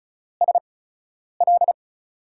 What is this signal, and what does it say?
Morse code, heard as a keyed tone, about 36 wpm, spaced as for 10 wpm, 700 Hz. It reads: SL
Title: USA Call Signs - 5 Characters - 8x Character Spacing - 35wpm